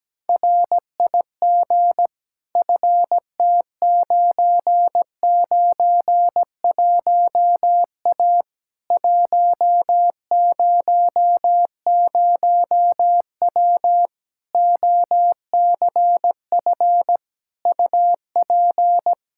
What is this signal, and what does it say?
Morse code, heard as a keyed tone, about 17 wpm, 700 Hz. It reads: RIG FT991A 100W OCF UP
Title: Practice QSO - Exchange 2 - Info - 17wpm